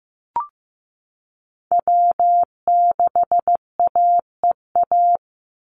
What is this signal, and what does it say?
Morse code, heard as a keyed tone, about 15 wpm, 700 Hz. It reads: W6AEA
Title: Call Signs Repeated 2x - 15wpm